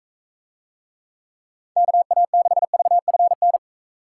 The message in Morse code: KA6VFN

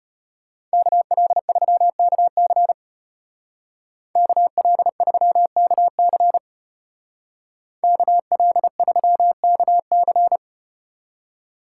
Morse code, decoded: KL3KC KL3KC KL3KC

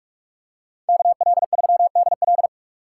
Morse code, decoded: KL3DL

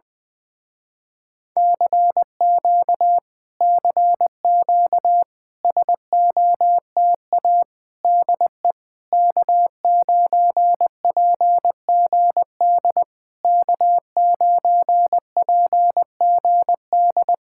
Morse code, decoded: CQ CQ SOTA DE K9PGD K9PGD

